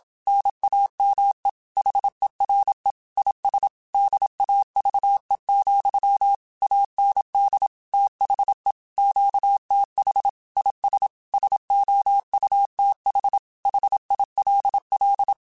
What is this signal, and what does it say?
Morse code, heard as a keyed tone, about 27 wpm, 800 Hz. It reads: NAME HERE IS DAVE, AND THE QTH IS SOUTH HILL